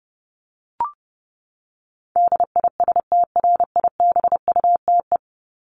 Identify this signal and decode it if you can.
Morse code, heard as a keyed tone, about 30 wpm, 700 Hz. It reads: DISTRIBUTE
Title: Top 1501-1750 Words 30wpm